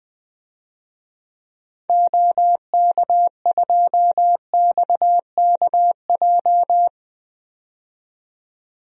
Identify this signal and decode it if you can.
Morse code, heard as a keyed tone, about 21 wpm, 700 Hz. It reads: OK2XKJ
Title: DXCC-100 Call Signs - 6 Characters - 20wpm